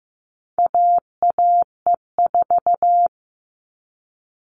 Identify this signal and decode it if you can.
Morse code, heard as a keyed tone, about 15 wpm, 700 Hz. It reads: AAE4
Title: All Cut Numbers - 4-Digit 15wpm